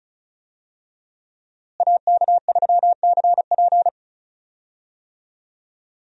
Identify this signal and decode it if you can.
Morse code, heard as a keyed tone, about 34 wpm, 700 Hz. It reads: AK3CP
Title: USA Call Signs - 5 Characters - Repeated 2x - 35wpm